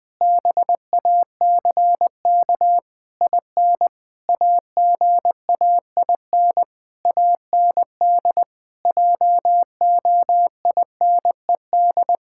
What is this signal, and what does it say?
Morse code, heard as a keyed tone, about 20 wpm, 700 Hz. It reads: BACK IN AGAIN AND JOINED
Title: Pangram Challenge v2 - 20wpm